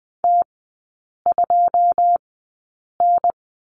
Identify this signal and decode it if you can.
Morse code, heard as a keyed tone, about 20 wpm, spaced as for 8 wpm, 700 Hz. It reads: T2N
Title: 081 Lesson 14e - Number 2 - Three Characters 20wpm